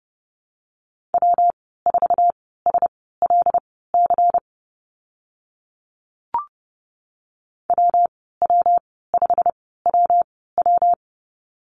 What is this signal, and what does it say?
Morse code, heard as a keyed tone, about 30 wpm, spaced as for 17 wpm, 700 Hz. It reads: W4SLC WW5WW
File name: Lesson 19g - Letter w - Call Signs 30wpm